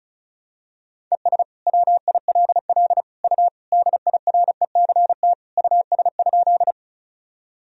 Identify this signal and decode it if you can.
Morse code, heard as a keyed tone, about 36 wpm, 700 Hz. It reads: ES WILL U DIRECT US?
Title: Sentences from Top 400 Words and Abbreviations 35wpm